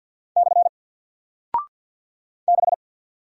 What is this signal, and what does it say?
Morse code, heard as a keyed tone, about 49 wpm, 700 Hz. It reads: / 6
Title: Single Character (Letter:Number:Punctuation:Prosign) Alt 50wpm